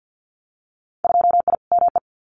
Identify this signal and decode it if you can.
Morse code, heard as a keyed tone, about 50 wpm, 700 Hz. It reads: 2I ME
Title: Winter Field Day Exchanges Encore 50wpm